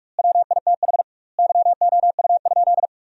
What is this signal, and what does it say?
Morse code, heard as a keyed tone, about 46 wpm, 700 Hz. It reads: WITH YOU?
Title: 2x Word-Spacing - Sentences from Top 100 Words - 45wpm